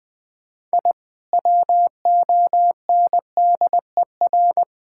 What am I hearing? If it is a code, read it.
Morse code, heard as a keyed tone, about 21 wpm, 700 Hz. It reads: I WONDER
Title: Sentences with 3-Word Phrases - 20wpm